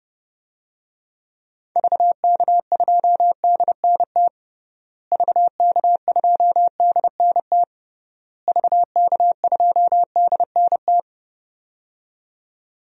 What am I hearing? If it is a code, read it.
Morse code, heard as a keyed tone, about 32 wpm, 700 Hz. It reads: VK2DNT VK2DNT VK2DNT
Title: DXCC-100 Call Signs - 6 Characters - Repeated 3x - 30wpm